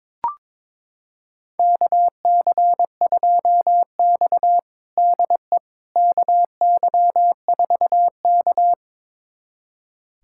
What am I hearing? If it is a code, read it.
Morse code, heard as a keyed tone, about 22 wpm, 700 Hz. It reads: KC2X DE KY4K
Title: USA Callsign DE Callsign - 4 Characters - 22wpm